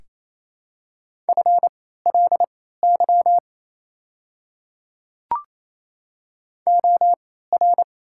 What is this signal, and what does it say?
Morse code, heard as a keyed tone, about 27 wpm, spaced as for 16 wpm, 700 Hz. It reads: FLY OR
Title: 3x Character Spacing - Top 500 Words - 28wpm